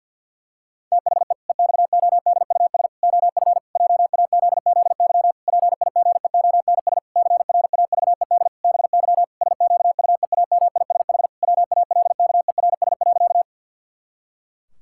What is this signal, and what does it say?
Morse code, heard as a keyed tone, about 50 wpm, 700 Hz. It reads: THE EXODUS OF JAZZY PIGEONS CRAVED BY SQUEAMISH WALKERS.
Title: Pangram Challenge v2 - 50wpm